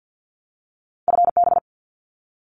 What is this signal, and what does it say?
Morse code, heard as a keyed tone, about 50 wpm, 700 Hz. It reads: FB
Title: Abbreviations 50wpm